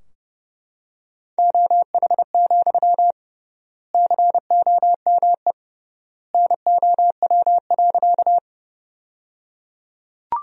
Morse code, heard as a keyed tone, about 30 wpm, 700 Hz. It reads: OH, COME NOW.
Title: 3x Word-Spacing - Sentences from Top 500 Words - 30wpm